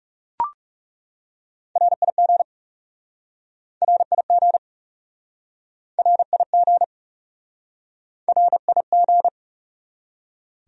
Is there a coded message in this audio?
Yes: Morse code from 1.8 to 9.3 s, heard as a keyed tone, about 40 wpm, 700 Hz. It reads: RIG RIG RIG RIG